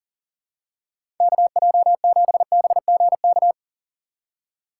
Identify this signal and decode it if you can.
Morse code, heard as a keyed tone, about 40 wpm, 700 Hz. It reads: KJ7BGK
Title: USA Call Signs - 4x Character Spacing - 40wpm